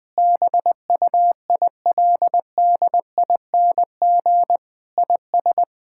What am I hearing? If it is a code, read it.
Morse code, heard as a keyed tone, about 20 wpm, 700 Hz. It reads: BUILDING IS
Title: Sentences from Top 1000 Words - Review Part 5 - 20wpm